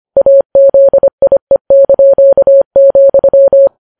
Morse code, heard as a keyed tone, about 25 wpm, 550 Hz. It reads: AZIE),